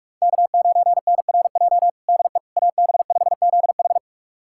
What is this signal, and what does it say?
Morse code, heard as a keyed tone, about 43 wpm, 700 Hz. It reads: K9NRJ DE AB5ZH
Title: USA Callsign DE Callsign - 5 Characters - 45wpm